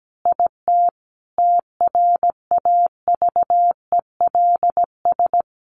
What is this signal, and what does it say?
Morse code, heard as a keyed tone, about 17 wpm, 700 Hz. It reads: IT TRAVELS
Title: Additional Easy Phrases 17wpm